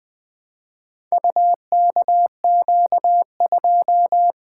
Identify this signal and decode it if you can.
Morse code, heard as a keyed tone, about 20 wpm, 700 Hz. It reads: UKQ2
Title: Four Letter-Number Combinations - 20wpm